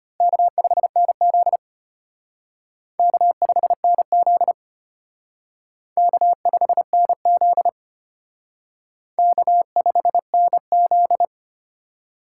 Morse code, heard as a keyed tone, about 32 wpm, 700 Hz. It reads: K5NZ K5NZ K5NZ K5NZ